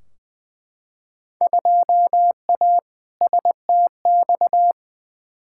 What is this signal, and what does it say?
Morse code, heard as a keyed tone, about 20 wpm, 700 Hz. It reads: 2A STX
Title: Field Day Exchanges - 20wpm